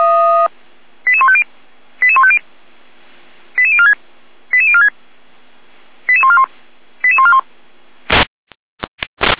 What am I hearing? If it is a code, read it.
Five-tone selective call (ZVEI), heard as quick runs of five tones, tones of about 70 ms. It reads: 88150 88150 88046 88046 88141 88141
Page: /455/SDR/ZVEI/88501/